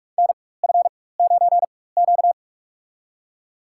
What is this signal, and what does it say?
Morse code, heard as a keyed tone, about 46 wpm, spaced as for 21 wpm, 700 Hz. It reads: NF9Q
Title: USA Call Signs - 4x Character Spacing - 45wpm